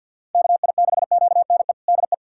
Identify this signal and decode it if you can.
Morse code, heard as a keyed tone, about 50 wpm, 700 Hz. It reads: KI6QNE DE
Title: USA Callsign DE Callsign 50wpm